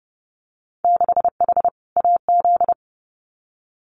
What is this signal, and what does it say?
Morse code, heard as a keyed tone, about 30 wpm, 700 Hz. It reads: 6H AZ
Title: Winter Field Day Exchanges Encore 30wpm